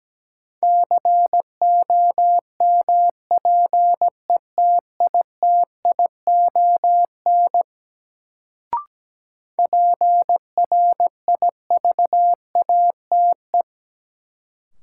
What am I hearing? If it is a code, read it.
Morse code, heard as a keyed tone, about 17 wpm, 700 Hz. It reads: COMPETITION PRIVATE
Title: Sentences from Top 1000 Words - New Word Drill - 17wpm